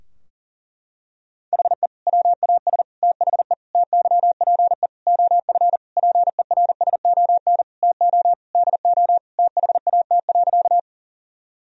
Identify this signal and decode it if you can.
Morse code, heard as a keyed tone, about 40 wpm, 700 Hz. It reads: HE WAS THE TYPE OF PERSON TO DO THAT.
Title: Sentences from Top 1000 Words - Review Part 3 - 40wpm